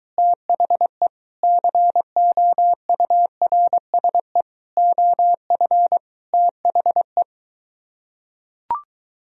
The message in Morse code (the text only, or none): THE COURSE OF THE